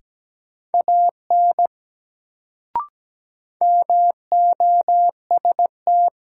Morse code, heard as a keyed tone, about 17 wpm, 700 Hz. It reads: AN MOST
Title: Sets of 2 Words - New Word Drill - 17wpm